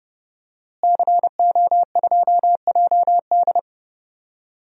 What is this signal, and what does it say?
Morse code, heard as a keyed tone, about 30 wpm, 700 Hz. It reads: CO2JD
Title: DXCC-100 Call Signs - 5 Characters - Repeated 3x - 30wpm